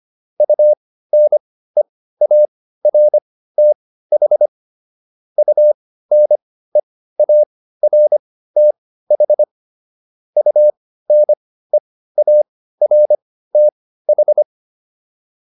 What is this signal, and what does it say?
Morse code, heard as a keyed tone, about 25 wpm, spaced as for 15 wpm, 600 Hz. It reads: UNEARTH UNEARTH UNEARTH